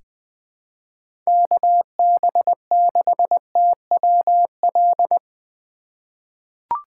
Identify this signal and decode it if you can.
Morse code, heard as a keyed tone, about 20 wpm, 700 Hz. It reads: KB6TWL
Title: USA Call Signs - 6 Characters - 3x Character Spacing - 20wpm